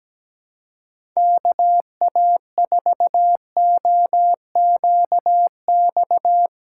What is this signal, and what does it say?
Morse code, heard as a keyed tone, about 17 wpm, 700 Hz. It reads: KA4OQX
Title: USA Call Signs - 6 Characters - 2x Character Spacing - 17wpm